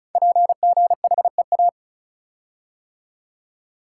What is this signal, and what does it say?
Morse code, heard as a keyed tone, about 36 wpm, 690 Hz. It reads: PGHEA
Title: Code Groups - 5 Letters - 35wpm